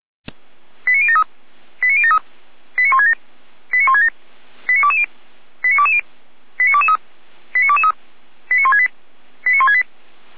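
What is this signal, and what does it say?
Five-tone selective call (ZVEI), heard as quick runs of five tones, tones of about 70 ms. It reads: 89073 89073 89167 89167 89220 89220 89223 89223 89168 89168
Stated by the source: Five-tone ZVEI selective call received by SDR.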